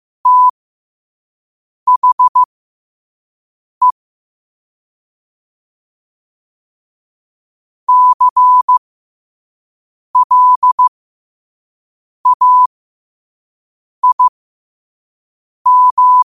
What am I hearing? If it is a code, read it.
Morse code, heard as a keyed tone, about 15 wpm, spaced as for 5 wpm, 1000 Hz. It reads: THE CLAIM
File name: qotdfiles / 5wpm